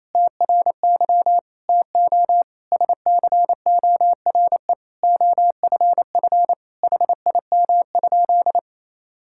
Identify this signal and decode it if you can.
Morse code, heard as a keyed tone, about 28 wpm, 700 Hz. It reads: TRY TO SCORE OFF HIM?
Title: Sentences from Top 1600 Words 28wpm